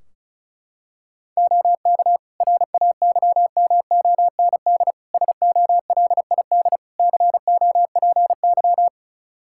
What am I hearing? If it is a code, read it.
Morse code, heard as a keyed tone, about 36 wpm, 700 Hz. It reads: OK RAYMOND SOLID COPY
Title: Practice QSO - Exchange 3 - More Info - 35wpm